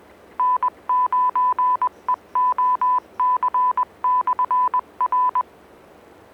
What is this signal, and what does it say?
Morse code, heard as a keyed tone, about 20 wpm, 1000 Hz. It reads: N9EOC/R